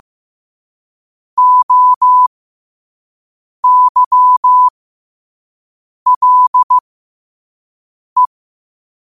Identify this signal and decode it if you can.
Morse code, heard as a keyed tone, about 15 wpm, spaced as for 5 wpm, 1000 Hz. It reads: OYLE
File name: qotdfiles / 5wpm